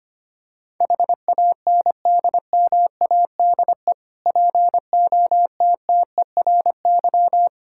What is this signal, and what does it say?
Morse code, heard as a keyed tone, about 25 wpm, 700 Hz. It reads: HANDMADE POTTERY
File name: practice / 25wpm / Difficult Words with Hints - 25wpm